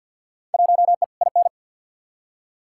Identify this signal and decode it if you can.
Morse code, heard as a keyed tone, about 50 wpm, 700 Hz. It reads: 1E IN
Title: Field Day Exchanges - 50wpm